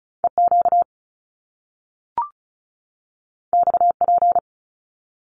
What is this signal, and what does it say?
Morse code, heard as a keyed tone, about 35 wpm, 700 Hz. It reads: EQ XP
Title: Top Two Letter Combinations in Words 35wpm